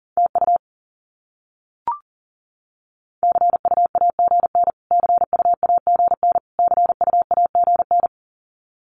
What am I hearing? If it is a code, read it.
Morse code, heard as a keyed tone, about 40 wpm, 700 Hz. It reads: TU CUAGN CUAGN CUAGN
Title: Abbreviations Repeated 3x - 40wpm